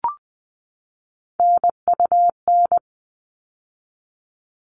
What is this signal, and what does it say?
Morse code, heard as a keyed tone, about 20 wpm, 700 Hz. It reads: NUN
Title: All Cut Numbers - 3-Digit 20wpm